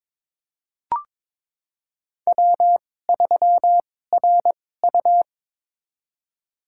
Morse code, heard as a keyed tone, about 22 wpm, spaced as for 16 wpm, 700 Hz. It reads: W3RU
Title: USA Call Signs - 4 Characters - 2x Character Spacing - 22wpm